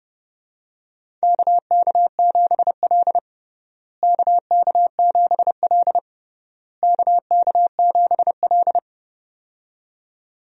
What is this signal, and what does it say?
Morse code, heard as a keyed tone, about 30 wpm, 700 Hz. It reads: KK7L KK7L KK7L
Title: USA Call Signs - 4 Characters - Repeated 3x - 30wpm